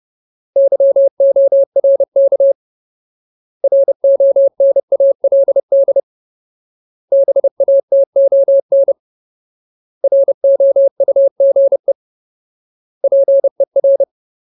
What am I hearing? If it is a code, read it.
Morse code, heard as a keyed tone, about 30 wpm, 550 Hz. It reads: YORK RONALD BATON ROUGE PER
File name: Ultimate Word List / 30WPM Ultimate